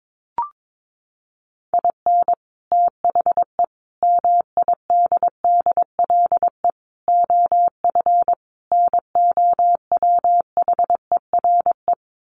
Morse code, heard as a keyed tone, about 22 wpm, 700 Hz. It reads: IN THE MIDDLE OF NOWHERE
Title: Sets of 5 Words Encore 22wpm